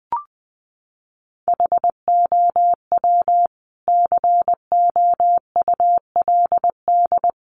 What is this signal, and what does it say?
Morse code, heard as a keyed tone, about 20 wpm, 700 Hz. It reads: HOW COULD
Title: Sentences from Top 200 Words 20wpm